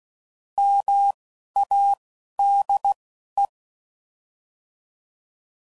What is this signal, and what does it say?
Morse code, heard as a keyed tone, about 16 wpm, spaced as for 11 wpm, 800 Hz. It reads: MADE